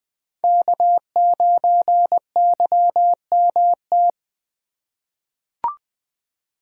Morse code, heard as a keyed tone, about 20 wpm, 700 Hz. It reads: K9YMT
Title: USA Call Signs - 5 Characters - 8x Character Spacing - 20wpm